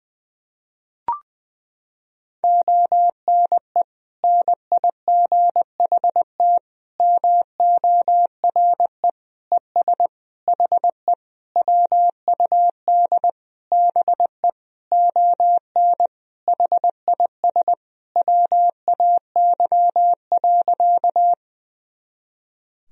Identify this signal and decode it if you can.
Morse code, heard as a keyed tone, about 21 wpm, 700 Hz. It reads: ONE NIGHT MORE ES HE WUD BE ON HIS WAY.
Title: Sentences from Top 300 Words and Abbreviations 20wpm